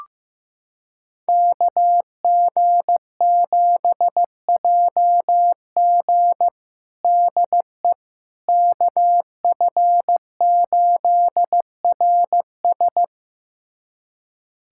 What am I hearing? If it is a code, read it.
Morse code, heard as a keyed tone, about 15 wpm, 700 Hz. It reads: KG7JG DE KF8RS
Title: USA Callsign DE Callsign - 5 Characters - 15wpm